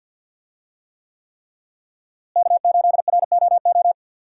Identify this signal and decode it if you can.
Morse code, heard as a keyed tone, about 47 wpm, 700 Hz. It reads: K8ROO
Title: USA Call Signs - 5 Characters - Repeated 3x - 50wpm